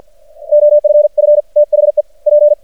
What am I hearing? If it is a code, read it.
Morse code, heard as a keyed tone, about 50 wpm, 590 Hz. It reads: OFF THE P
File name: archived / CRSnet